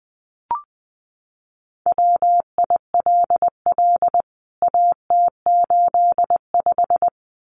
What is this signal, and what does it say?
Morse code, heard as a keyed tone, about 20 wpm, 700 Hz. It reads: WILL AT85
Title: CWT Exchange 20wpm